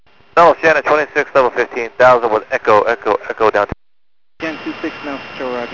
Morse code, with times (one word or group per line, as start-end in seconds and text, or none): none